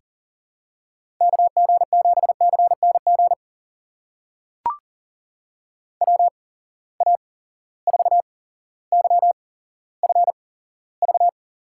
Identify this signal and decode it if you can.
Morse code, heard as a keyed tone, about 40 wpm, spaced as for 17 wpm, 700 Hz. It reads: KG7CNG WA4YFV
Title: USA Call Signs - 6 Characters - 8x Character Spacing - 40wpm